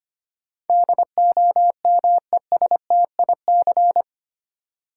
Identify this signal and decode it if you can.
Morse code, heard as a keyed tone, about 25 wpm, 700 Hz. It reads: DOMESTIC
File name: 8 Letter Words - First 3 Letters Given - 25wpm